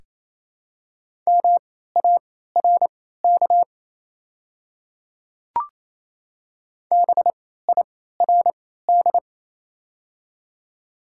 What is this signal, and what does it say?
Morse code, heard as a keyed tone, about 27 wpm, spaced as for 16 wpm, 690 Hz. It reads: MARK BIRD
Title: 3x Character Spacing - Top 500 Words - 28wpm